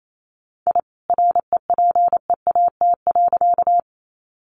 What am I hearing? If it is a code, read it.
Morse code, heard as a keyed tone, about 28 wpm, 700 Hz. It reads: I REPEAT.
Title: A Sound Of Thunder 28wpm